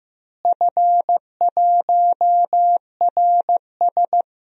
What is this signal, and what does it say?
Morse code, heard as a keyed tone, about 15 wpm, 700 Hz. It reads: F1RS
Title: DXCC-100 Call Signs - 4 Characters - Repeated 2x - 15wpm